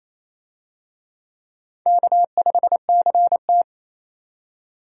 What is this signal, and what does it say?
Morse code, heard as a keyed tone, about 28 wpm, 700 Hz. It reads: K5CT
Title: USA Call Signs - 4 Characters - 28wpm